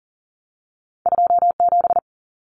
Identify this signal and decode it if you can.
Morse code, heard as a keyed tone, about 40 wpm, 700 Hz. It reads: 27
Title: Common Cut Numbers - 2-Digit 40wpm